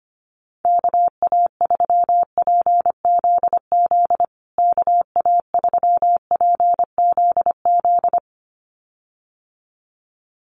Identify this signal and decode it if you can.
Morse code, heard as a keyed tone, about 25 wpm, 700 Hz. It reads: KA3PZZ KA3PZZ
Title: USA Call Signs Repeated 2x 25wpm